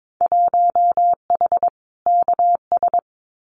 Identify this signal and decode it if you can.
Morse code, heard as a keyed tone, about 22 wpm, 700 Hz. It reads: 1H KS
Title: Winter Field Day Exchanges 22wpm